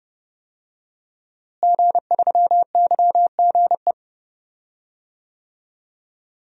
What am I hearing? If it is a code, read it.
Morse code, heard as a keyed tone, about 30 wpm, 700 Hz. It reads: G3YGE
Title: DXCC-100 Call Signs - 5 Characters - Repeated 2x - 30wpm